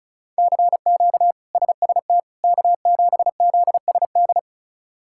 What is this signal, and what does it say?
Morse code, heard as a keyed tone, about 34 wpm, 700 Hz. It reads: CQ SST K7ZSD
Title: K1USN SST QSO Practice - 35wpm